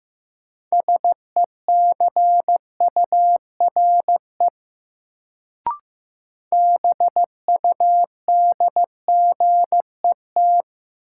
Morse code, heard as a keyed tone, about 15 wpm, 700 Hz. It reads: SECURE BUDGET
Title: Sentences from Top 1200 Words - New Word Drill Alt - 15wpm